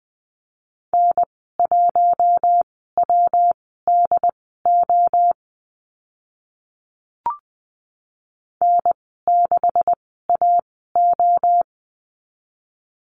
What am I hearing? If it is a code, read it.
Morse code, heard as a keyed tone, about 20 wpm, spaced as for 14 wpm, 700 Hz. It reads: N1WDO N6AO
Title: 161 Lesson 26g - Letter g - Call Signs 20wpm